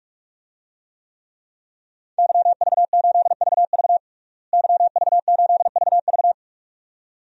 Morse code, heard as a keyed tone, about 46 wpm, 700 Hz. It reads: YV8VV YV8VV
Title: DXCC-100 Call Signs - 5 Characters - Repeated 2x - 45wpm